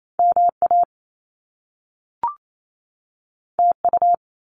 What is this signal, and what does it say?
Morse code, heard as a keyed tone, about 28 wpm, 700 Hz. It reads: MA TU
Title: Top Two Letter Combinations in Words 28wpm